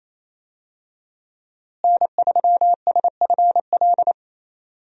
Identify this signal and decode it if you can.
Morse code, heard as a keyed tone, about 28 wpm, 700 Hz. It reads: N3SFL
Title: USA Call Signs - 5 Characters - Repeated 3x - 28wpm